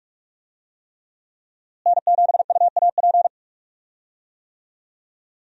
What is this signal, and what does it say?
Morse code, heard as a keyed tone, about 46 wpm, 700 Hz. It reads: N7UAP
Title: USA Call Signs - 5 Characters - Repeated 3x - 45wpm